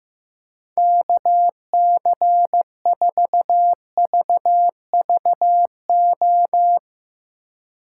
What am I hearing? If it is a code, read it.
Morse code, heard as a keyed tone, about 15 wpm, 700 Hz. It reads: KC4VVO
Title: Speed Racing - USA 2x3 Call Signs 15wpm